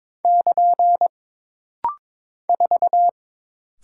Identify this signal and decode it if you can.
Morse code, heard as a keyed tone, about 22 wpm, 700 Hz. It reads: <KN> 4
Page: This rapid-fire timing is geared to push your ICR (Instant Character Recognition) skills.